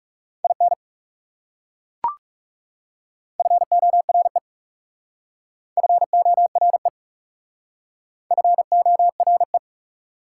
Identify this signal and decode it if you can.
Morse code, heard as a keyed tone, about 39 wpm, 690 Hz. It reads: IN FORE FORE FORE